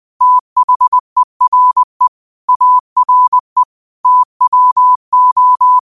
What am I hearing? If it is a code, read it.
Morse code, heard as a keyed tone, about 20 wpm, 1000 Hz. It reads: THERE ARE TWO
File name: qotdfiles / 20wpm